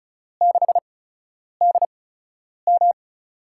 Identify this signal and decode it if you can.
Morse code, heard as a keyed tone, about 34 wpm, spaced as for 10 wpm, 700 Hz. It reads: 6DM
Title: USA Call Signs - 5 Characters - 8x Character Spacing - 35wpm